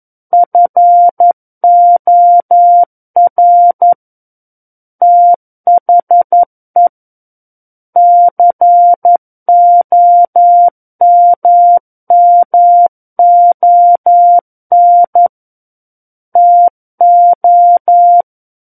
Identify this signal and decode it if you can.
Morse code, heard as a keyed tone, about 11 wpm, 700 Hz. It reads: FOR THE COMMON TO
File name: 11 WPM